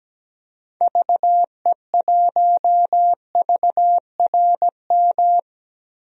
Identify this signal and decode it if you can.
Morse code, heard as a keyed tone, about 17 wpm, 700 Hz. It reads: VE1VRM
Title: DXCC-100 Call Signs - 6 Characters - 17wpm